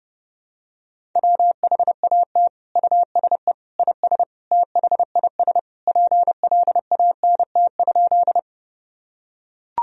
Morse code, heard as a keyed tone, about 30 wpm, 700 Hz. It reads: WHAT USE IS THIS PLANT?
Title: Sentences from Top 1000 Words - Review Part 1 - 30wpm